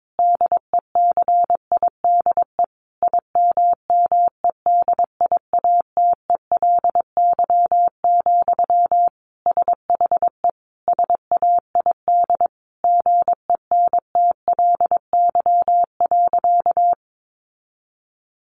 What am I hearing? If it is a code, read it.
Morse code, heard as a keyed tone, about 22 wpm, 700 Hz. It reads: DECIDE IMMEDIATELY, SHE SAID GENTLY.
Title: Sentences from Top 1500 Words 22wpm